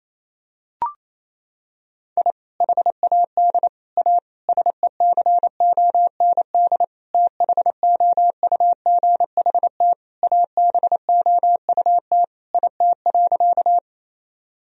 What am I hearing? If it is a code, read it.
Morse code, heard as a keyed tone, about 28 wpm, 700 Hz. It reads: I HAD A SECOND THOUGHT ABOUT IT.
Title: Sentences from Top 1000 Words - Review Part 1 - 28wpm